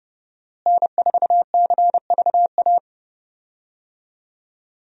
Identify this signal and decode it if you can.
Morse code, heard as a keyed tone, about 29 wpm, 700 Hz. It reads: N4CVA
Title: USA Call Signs - 5 Characters - Repeated 3x - 30wpm